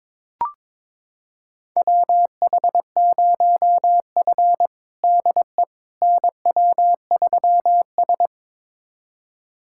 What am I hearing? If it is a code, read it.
Morse code, heard as a keyed tone, about 22 wpm, 700 Hz. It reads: WH0F DE NW3S
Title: USA Callsign DE Callsign - 4 Characters - 22wpm